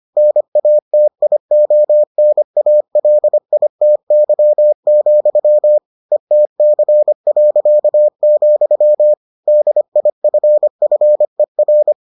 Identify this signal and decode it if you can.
Morse code, heard as a keyed tone, about 25 wpm, 600 Hz. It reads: NATIONALITY, ETC., DIFFER